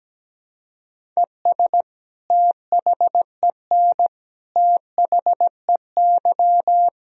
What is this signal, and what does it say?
Morse code, heard as a keyed tone, about 17 wpm, 700 Hz. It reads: ES THEN THEY
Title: Sentences from Top 300 Words and Abbreviations 17wpm